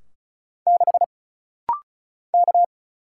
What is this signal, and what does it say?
Morse code, heard as a keyed tone, about 36 wpm, 700 Hz. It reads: 6 K
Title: Single Letter-Number Alt - Rapid-Fire 35wpm